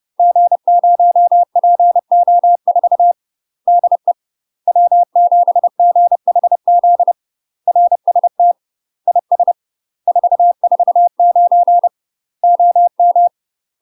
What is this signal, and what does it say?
Morse code, heard as a keyed tone, about 30 wpm, 700 Hz. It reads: G0PO4 DE W7GHZ RST IS 449 OM
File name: UK MASTER QSOs TRAINING / 30WPM / UK MASTER QSOs 0005